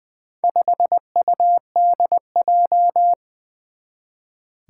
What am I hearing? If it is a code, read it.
Morse code, heard as a keyed tone, about 20 wpm, 700 Hz. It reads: 5UDJ
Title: Four Challenging Letter-Number Combinations 20wpm